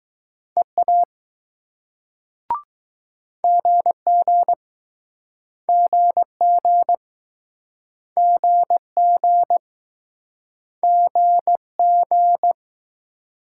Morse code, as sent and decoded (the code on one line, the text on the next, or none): . .- / --. --. / --. --. / --. --. / --. --.
EA GG GG GG GG